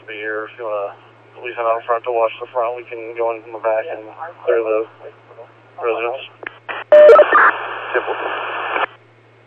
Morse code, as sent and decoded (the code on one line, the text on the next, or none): none